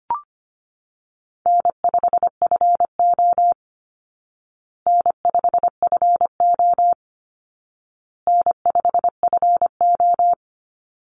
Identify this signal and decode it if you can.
Morse code, heard as a keyed tone, about 25 wpm, 700 Hz. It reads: N5FO N5FO N5FO
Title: Easy Call Signs Repeated 3x - 25wpm